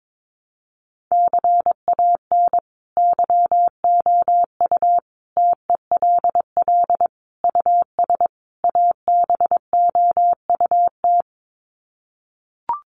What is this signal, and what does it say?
Morse code, heard as a keyed tone, about 22 wpm, 700 Hz. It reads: CAN YOU TELL US ABOUT
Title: Sets of 5 Words Encore2 22wpm